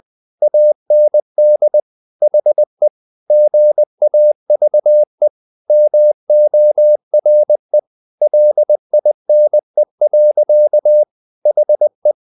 Morse code, heard as a keyed tone, about 20 wpm, 600 Hz. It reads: AND HE GAVE MORE LINE. HE
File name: the old man and the sea / the old man and the sea part 1 20wpm